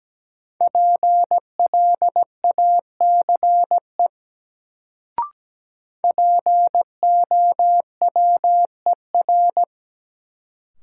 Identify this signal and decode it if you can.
Morse code, heard as a keyed tone, about 17 wpm, 700 Hz. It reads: PLACE POWER
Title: Sets of 4 Words - New Word Drill - 17wpm